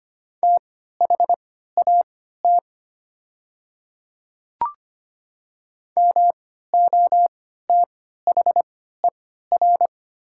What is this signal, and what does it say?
Morse code, heard as a keyed tone, about 25 wpm, spaced as for 14 wpm, 700 Hz. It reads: THAT MOTHER
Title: 3x Character Spacing - Top 300 Words - 25wpm